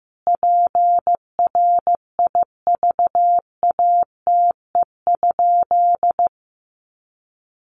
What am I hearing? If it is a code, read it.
Morse code, heard as a keyed tone, about 15 wpm, 700 Hz. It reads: PRIVATE?
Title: Sentences from Top 1000 Words 15wpm